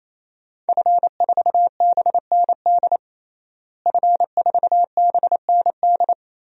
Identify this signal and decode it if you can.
Morse code, heard as a keyed tone, about 27 wpm, 700 Hz. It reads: F4BND F4BND
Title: DXCC-100 Call Signs - 5 Characters - Repeated 3x - 28wpm